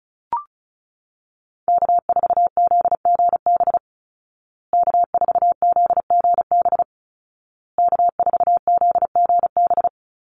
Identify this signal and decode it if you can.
Morse code, heard as a keyed tone, about 34 wpm, 700 Hz. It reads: K4ZGB K4ZGB K4ZGB
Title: Call Signs Repeated 2x - 35wpm